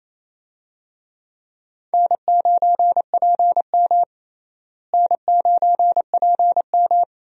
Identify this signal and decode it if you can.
Morse code, heard as a keyed tone, about 27 wpm, 700 Hz. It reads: N9PM N9PM
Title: USA Call Signs - 4 Characters - Repeated 2x - 28wpm